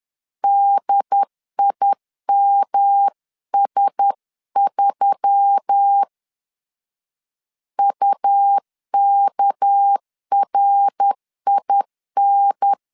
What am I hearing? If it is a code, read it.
Morse code, heard as a keyed tone, about 10 wpm, 800 Hz. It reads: DIMS3 UKRIN
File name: LEZIONI / 8 YJ45 / 8 YJ4 10wpm 2633